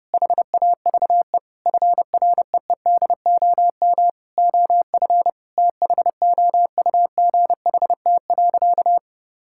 Morse code, heard as a keyed tone, about 30 wpm, 700 Hz. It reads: HAVE FREEDOM OF THOUGHT.
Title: Sentences from Top 1000 Words - Review Part 5 - 30wpm